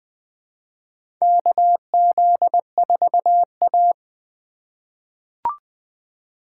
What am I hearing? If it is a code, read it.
Morse code, heard as a keyed tone, about 20 wpm, 700 Hz. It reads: KZ4A
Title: K1USN SST QSO Practice - 20wpm